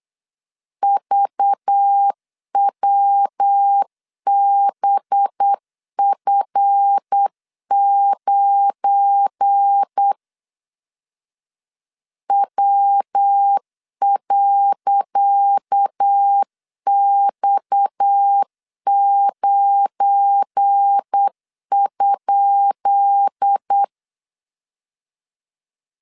Morse code, heard as a keyed tone, about 8 wpm, 800 Hz. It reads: VWBF9 W.X9?
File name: LEZIONI / 10 SIGNS / 10C DOT 8wpm 3649